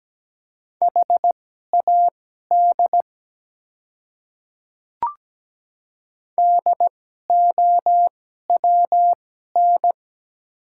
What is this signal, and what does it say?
Morse code, heard as a keyed tone, about 17 wpm, spaced as for 12 wpm, 700 Hz. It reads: HAD DOWN
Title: 2x Character Spacing - Top 100 Words - 17wpm